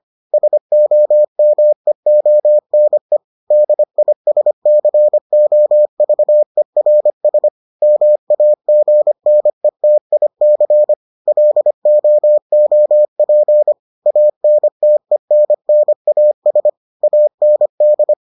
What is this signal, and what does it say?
Morse code, heard as a keyed tone, about 25 wpm, 600 Hz. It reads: SOMEONE DISCOVERS MAGNETIC LOOP ANTENNAS AND